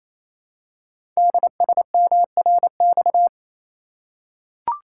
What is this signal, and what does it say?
Morse code, heard as a keyed tone, about 28 wpm, 700 Hz. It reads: DSMRX
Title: Code Groups - 5 Letters - 28wpm